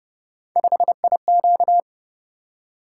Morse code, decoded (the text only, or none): 5IQ